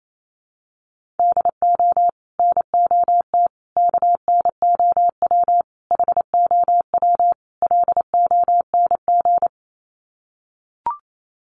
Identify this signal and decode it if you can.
Morse code, heard as a keyed tone, about 28 wpm, 700 Hz. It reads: DO NOT KNOW HOW LONG
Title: Sets of 5 Words Encore2 28wpm